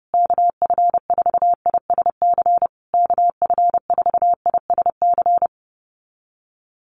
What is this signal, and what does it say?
Morse code, heard as a keyed tone, about 30 wpm, 700 Hz. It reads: KF4ISC KF4ISC
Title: USA Call Signs Repeated 3x 30wpm